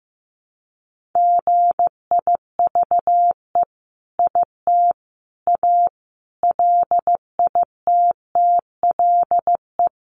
Morse code, heard as a keyed tone, about 15 wpm, 700 Hz. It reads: GIVE IT A LITTLE
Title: Sentences from Top 1500 Words 15wpm